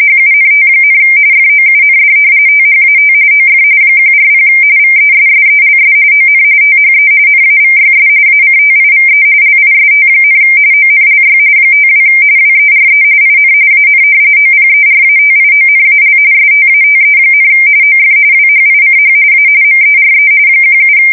RTTY, about 45.45 baud. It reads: HUNT, FOUND THAT OF 140 VESSELS ON THE NAVYS ACTIVE LIST, ONLY 52 WERE IN AN OPERATIONAL STATE, OF WHICH A MERE 17
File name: fsk170 / usnavy3